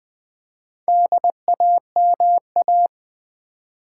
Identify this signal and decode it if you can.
Morse code, heard as a keyed tone, about 20 wpm, 700 Hz. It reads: DAMA
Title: Build A Word - 2 Letters At A Time - 8-Letter Words - 20wpm